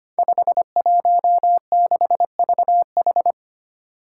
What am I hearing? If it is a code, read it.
Morse code, heard as a keyed tone, about 25 wpm, 700 Hz. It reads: 516VH
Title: Five Challenging Letter-Number Combinations 25wpm